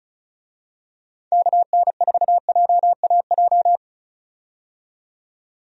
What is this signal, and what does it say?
Morse code, heard as a keyed tone, about 34 wpm, 700 Hz. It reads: KN4JAJ
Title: USA Call Signs - 6 Characters - 35wpm